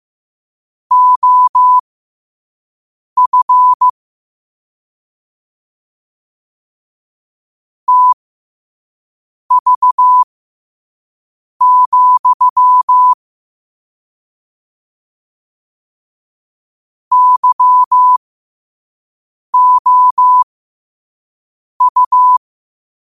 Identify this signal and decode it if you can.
Morse code, heard as a keyed tone, about 15 wpm, spaced as for 5 wpm, 1000 Hz. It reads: OF TV, YOU
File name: qotdfiles / 5wpm